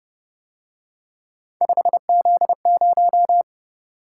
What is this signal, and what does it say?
Morse code, heard as a keyed tone, about 31 wpm, 700 Hz. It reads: 5Z0